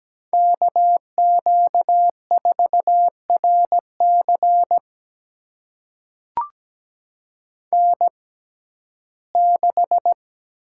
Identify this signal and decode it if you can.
Morse code, heard as a keyed tone, about 17 wpm, spaced as for 12 wpm, 700 Hz. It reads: KQ4RC N6
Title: USA Call Signs - 5 Characters - 6x Character Spacing - 17wpm